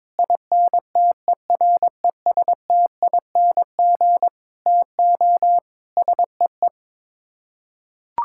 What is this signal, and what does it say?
Morse code, heard as a keyed tone, about 22 wpm, 700 Hz. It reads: INTERESTING TO SEE
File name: practice / 22wpm / Sets of 7 Words - 22wpm